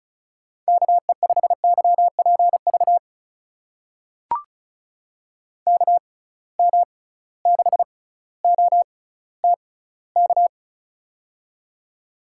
Morse code, heard as a keyed tone, about 36 wpm, spaced as for 18 wpm, 700 Hz. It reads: KE5YPV KM6OTK
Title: USA Call Signs - 6 Characters - 6x Character Spacing - 35wpm